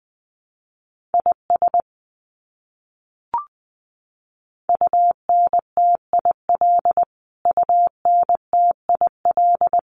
Morse code, heard as a keyed tone, about 19 wpm, 700 Hz. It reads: IS UNTIL UNTIL
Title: Top 500 Words Repeated 20wpm